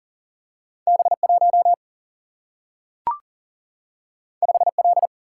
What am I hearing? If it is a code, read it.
Morse code, heard as a keyed tone, about 42 wpm, 700 Hz. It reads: B 1 5 L
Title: Single-Letter Single-Number 40wpm